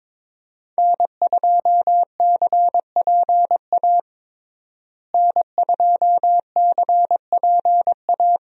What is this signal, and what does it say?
Morse code, heard as a keyed tone, about 22 wpm, 700 Hz. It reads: N2CPA N2CPA
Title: USA Call Signs - 5 Characters - Repeated 3x - 22wpm